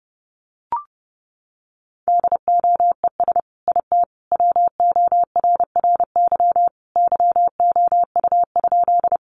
Morse code, heard as a keyed tone, about 30 wpm, 700 Hz. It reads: DOES IT WORRY YOU?
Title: Sentences from Top 1100 Words 30wpm